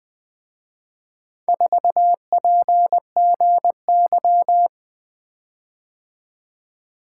Morse code, heard as a keyed tone, about 24 wpm, 700 Hz. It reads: 4PGY
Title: Speed-Racing - Four Letter-Number Combinations - 20wpm